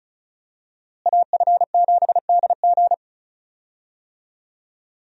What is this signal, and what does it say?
Morse code, heard as a keyed tone, about 36 wpm, 700 Hz. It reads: AF7DG
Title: USA Call Signs - 5 Characters - Repeated 2x - 35wpm